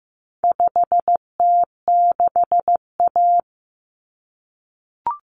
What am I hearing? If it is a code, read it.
Morse code, heard as a keyed tone, about 15 wpm, 700 Hz. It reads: 5T6A
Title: Common Cut Numbers - 4-Digit 15wpm